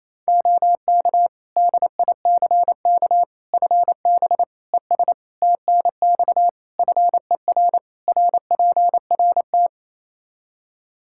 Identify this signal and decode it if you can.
Morse code, heard as a keyed tone, about 28 wpm, 700 Hz. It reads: OK DICK FB ES TNX FER RPRT
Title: Practice QSO - Exchange 2 - Info - 28wpm